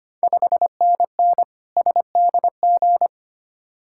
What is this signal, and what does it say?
Morse code, heard as a keyed tone, about 25 wpm, 690 Hz. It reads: 5NN SDG
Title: Ontario QSO Party Exchange 25wpm